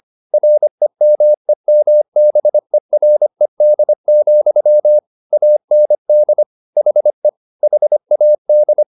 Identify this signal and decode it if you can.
Morse code, heard as a keyed tone, about 25 wpm, 600 Hz. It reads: REMEMBERED, AND HE HAD
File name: the old man and the sea part 1 25wpm